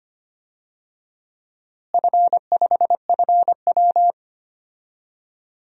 Morse code, heard as a keyed tone, about 25 wpm, 700 Hz. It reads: F5FW
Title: DXCC-100 Call Signs - 4 Characters - Repeated 3x - 25wpm